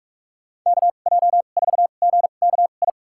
Morse code, heard as a keyed tone, about 46 wpm, spaced as for 33 wpm, 700 Hz. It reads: KJ4GKI
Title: USA Call Signs - 6 Characters - 2x Character Spacing - 45wpm